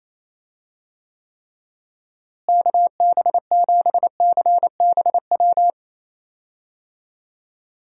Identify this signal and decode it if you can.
Morse code, heard as a keyed tone, about 28 wpm, 700 Hz. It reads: KB7CBW
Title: USA Call Signs - 6 Characters - Repeated 3x - 28wpm